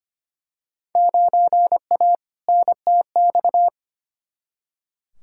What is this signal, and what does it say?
Morse code, heard as a keyed tone, about 25 wpm, 700 Hz. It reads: 9A NTX
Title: Field Day Exchanges Encore - 25wpm